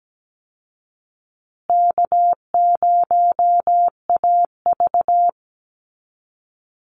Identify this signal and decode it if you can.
Morse code, heard as a keyed tone, about 17 wpm, 700 Hz. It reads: K0AV
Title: Easy Call Signs Repeated 3x - 17wpm